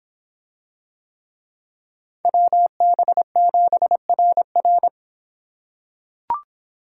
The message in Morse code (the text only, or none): WB7RR